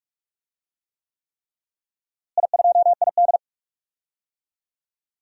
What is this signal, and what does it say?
Morse code, heard as a keyed tone, about 46 wpm, 700 Hz. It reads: I2ID
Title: DXCC-100 Call Signs - 4 Characters - Repeated 3x - 45wpm